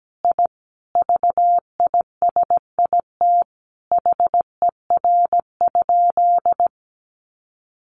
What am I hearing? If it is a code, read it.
Morse code, heard as a keyed tone, about 17 wpm, 700 Hz. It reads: I VISIT HER?
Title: Sentences from Top 900 Words 17wpm